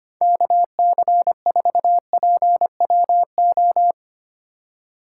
Morse code, heard as a keyed tone, about 31 wpm, 700 Hz. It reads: KC4PWO
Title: Speed Racing - USA 2x3 Call Signs 25wpm